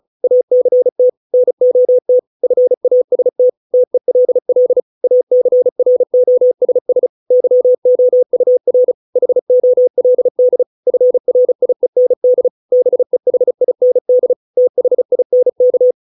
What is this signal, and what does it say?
Morse code, heard as a keyed tone, about 34 wpm, 500 Hz. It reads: ACT NOT FAST TELL ACROSS YOUR HOLD FRIEND BEHIND THINK